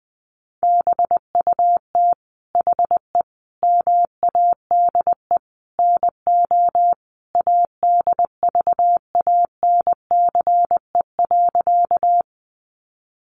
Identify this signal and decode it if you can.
Morse code, heard as a keyed tone, about 20 wpm, 700 Hz. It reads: BUT HE MADE NO ADVANCE.
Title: Sentences from Top 1400 Words 20wpm